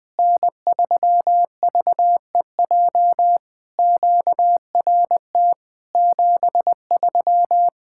Morse code, heard as a keyed tone, about 20 wpm, 700 Hz. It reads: N3VEJ QRT 73
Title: POTA QSO Practice - 20wpm